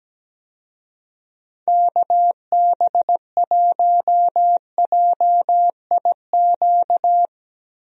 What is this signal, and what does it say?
Morse code, heard as a keyed tone, about 17 wpm, 700 Hz. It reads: KB1JIQ
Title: USA Call Signs - 6 Characters - 17wpm